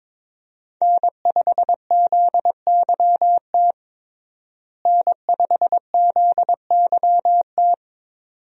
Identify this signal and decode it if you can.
Morse code, heard as a keyed tone, about 22 wpm, 700 Hz. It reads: N5ZYT N5ZYT
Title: USA Call Signs - 5 Characters - Repeated 2x - 22wpm